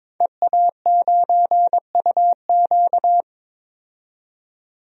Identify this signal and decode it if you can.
Morse code, heard as a keyed tone, about 21 wpm, 700 Hz. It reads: EA9UQ
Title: DXCC-100 Call Signs - 5 Characters - 22wpm